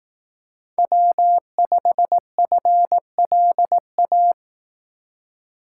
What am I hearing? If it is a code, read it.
Morse code, heard as a keyed tone, about 18 wpm, 700 Hz. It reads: W5FLA
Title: Speed Racing - USA 1x3 Call Signs 15wpm